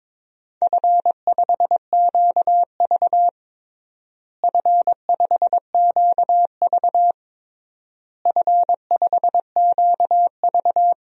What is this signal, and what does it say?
Morse code, heard as a keyed tone, about 22 wpm, 700 Hz. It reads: F5QV F5QV F5QV
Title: DXCC-100 Call Signs - 4 Characters - Repeated 3x - 22wpm